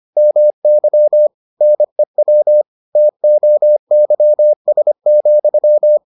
Morse code, heard as a keyed tone, about 25 wpm, 600 Hz. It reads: MY NEW TOYS,